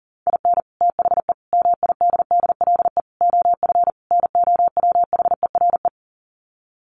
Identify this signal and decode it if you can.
Morse code, heard as a keyed tone, about 40 wpm, 700 Hz. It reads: IN THE MIDDLE OF NOWHERE
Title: Sets of 5 Words Encore 40wpm